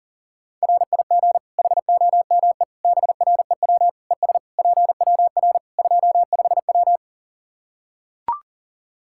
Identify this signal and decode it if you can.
Morse code, heard as a keyed tone, about 40 wpm, 700 Hz. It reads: RIG HOME BREW ES PWR 25W
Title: Practice QSO - Exchange 2 - Info - 40wpm